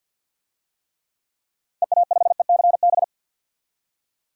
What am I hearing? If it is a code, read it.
Morse code, heard as a keyed tone, about 52 wpm, 700 Hz. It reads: EA5ECB
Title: DXCC-100 Call Signs - 6 Characters - 50wpm